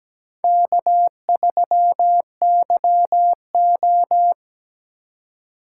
Morse code, heard as a keyed tone, about 20 wpm, 700 Hz. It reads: K3YO
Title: Speed Racing - USA 1x2 Call Signs 17wpm